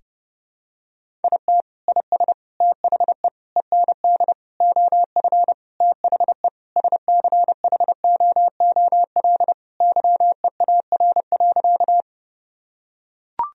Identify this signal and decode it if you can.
Morse code, heard as a keyed tone, about 30 wpm, 700 Hz. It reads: IT IS THE END OF THE SCHOOL YEAR.
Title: Sentences from Top 1000 Words - Review Part 1 - 30wpm